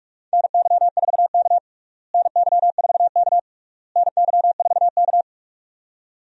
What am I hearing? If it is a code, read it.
Morse code, heard as a keyed tone, about 46 wpm, 700 Hz. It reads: NY4K NY4K NY4K
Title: USA Call Signs - 4 Characters - Repeated 3x - 45wpm